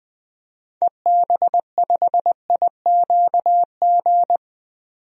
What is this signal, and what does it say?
Morse code, heard as a keyed tone, about 20 wpm, 700 Hz. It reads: EB5IQG
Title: DXCC-100 Call Signs - 6 Characters - 20wpm